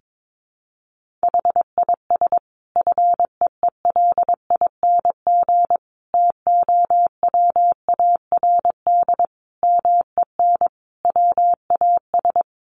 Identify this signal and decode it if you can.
Morse code, heard as a keyed tone, about 22 wpm, 700 Hz. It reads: HIS FEELING TOWARD MEN WAS
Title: Sentences from Top 1300 Words 22wpm